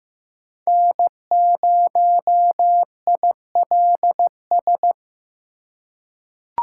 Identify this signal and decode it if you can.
Morse code, heard as a keyed tone, about 15 wpm, 700 Hz. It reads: N0ILS
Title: USA Call Signs - 2x Character Spacing - 15wpm